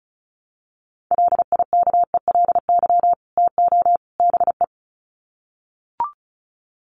Morse code, heard as a keyed tone, about 34 wpm, 700 Hz. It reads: LIKELY TO BE
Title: Sets of 3 Words 35wpm